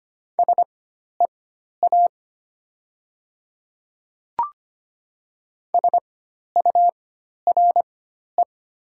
Morse code, heard as a keyed tone, about 25 wpm, spaced as for 12 wpm, 700 Hz. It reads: SEA SURE
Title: 4x Character Spacing - Top 300 Words - 25wpm